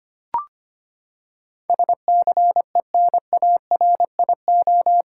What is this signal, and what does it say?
Morse code, heard as a keyed tone, about 25 wpm, 700 Hz. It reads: SCENARIO
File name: Eight Letter Words - 25wpm